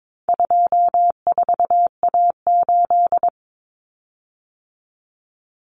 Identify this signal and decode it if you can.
Morse code, heard as a keyed tone, about 22 wpm, 700 Hz. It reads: 24A8
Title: Common Cut Numbers - 4-Digit 22wpm